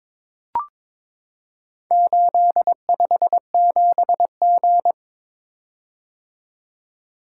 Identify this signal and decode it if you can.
Morse code, heard as a keyed tone, about 22 wpm, 700 Hz. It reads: 857G
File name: practice / 22wpm / Four Letter-Number Combinations - 22wpm